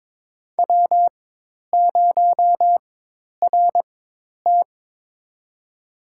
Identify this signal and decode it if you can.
Morse code, heard as a keyed tone, about 22 wpm, spaced as for 10 wpm, 700 Hz. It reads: W0RT
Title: USA Call Signs - 4 Characters - 4x Character Spacing - 22wpm